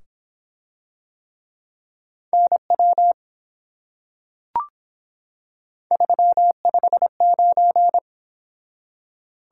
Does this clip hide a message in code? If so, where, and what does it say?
Yes: Morse code from 2.3 to 8.0 s, heard as a keyed tone, about 25 wpm, 700 Hz. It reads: NW 359